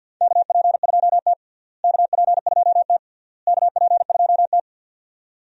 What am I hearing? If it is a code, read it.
Morse code, heard as a keyed tone, about 52 wpm, 700 Hz. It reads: KP2T KP2T KP2T
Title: DXCC-100 Call Signs - 4 Characters - Repeated 3x - 50wpm